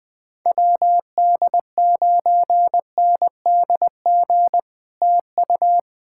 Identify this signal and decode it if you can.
Morse code, heard as a keyed tone, about 20 wpm, 700 Hz. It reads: WD9NDG TU
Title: POTA QSO Practice - 20wpm